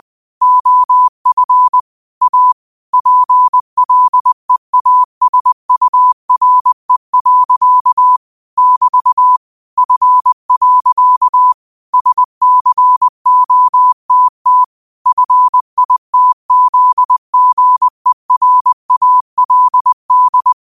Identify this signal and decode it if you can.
Morse code, heard as a keyed tone, about 20 wpm, 1000 Hz. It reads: OF A PLEASURE. <BT> F. SCOTT FITZGERALD
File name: qotdfiles / 20wpm